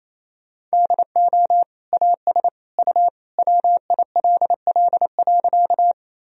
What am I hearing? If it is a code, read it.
Morse code, heard as a keyed tone, about 28 wpm, 700 Hz. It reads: DO AS U WILL.
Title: Sentences from Top 100 Words and Abbreviations 28wpm